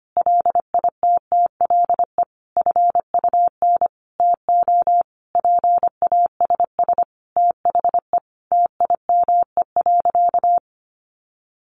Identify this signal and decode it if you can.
Morse code, heard as a keyed tone, about 25 wpm, 700 Hz. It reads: LITTLE FUN TO PASS THE TIME.
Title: Sentences from Top 700 Words 25wpm